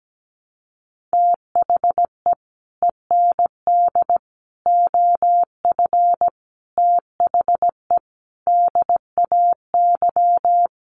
Morse code, heard as a keyed tone, about 17 wpm, 700 Hz. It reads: THE END OF THE DAY
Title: Sets of 5 Words 17wpm